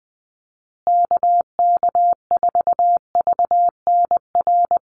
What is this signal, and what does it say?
Morse code, heard as a keyed tone, about 20 wpm, 700 Hz. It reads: KK4VNR
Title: USA Call Signs Repeated 2x 20wpm